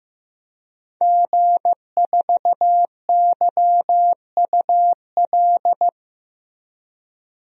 Morse code, heard as a keyed tone, about 15 wpm, 700 Hz. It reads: G4YUL
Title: DXCC-100 Call Signs - 5 Characters - Repeated 3x - 15wpm